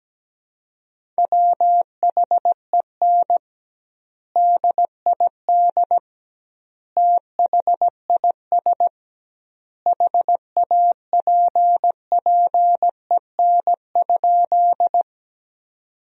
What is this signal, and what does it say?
Morse code, heard as a keyed tone, about 17 wpm, 700 Hz. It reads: WHEN DID THIS HAPPEN?
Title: 2x Word-Spacing - Sentences from Top 400 Words - 17wpm